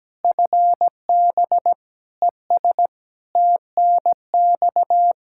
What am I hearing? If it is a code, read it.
Morse code, heard as a keyed tone, about 17 wpm, 700 Hz. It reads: FB ES TNX
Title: Practice QSO - Exchange 2 - Info - 17wpm